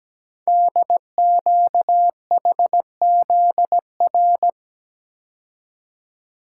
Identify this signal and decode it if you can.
Morse code, heard as a keyed tone, about 17 wpm, 700 Hz. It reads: DQHZR
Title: Code Groups - 5 Letter:Numbers - 17wpm